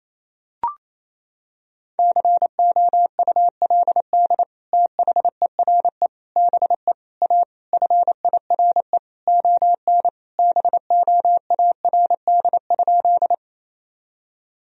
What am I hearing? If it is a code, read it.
Morse code, heard as a keyed tone, about 28 wpm, 700 Hz. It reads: COULD THERE BE A FIRE ON BOARD?
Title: Sentences from Top 1000 Words - Review Part 4 - 28wpm